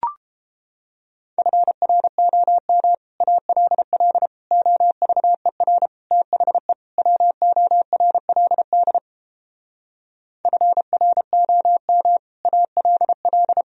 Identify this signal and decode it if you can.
Morse code, heard as a keyed tone, about 29 wpm, 700 Hz. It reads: FROM ALL OVER THE WORLD FROM ALL